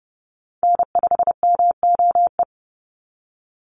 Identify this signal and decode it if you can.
Morse code, heard as a keyed tone, about 30 wpm, 700 Hz. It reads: N5MOE
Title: USA Call Signs 30wpm